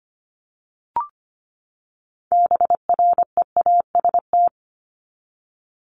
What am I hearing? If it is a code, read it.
Morse code, heard as a keyed tone, about 25 wpm, 700 Hz. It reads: BREAST
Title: Top 1751-2000 Words 25wpm